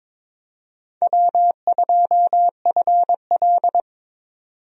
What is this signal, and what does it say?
Morse code, heard as a keyed tone, about 22 wpm, 700 Hz. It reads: W2FL
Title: USA Call Signs - 4 Characters - 22wpm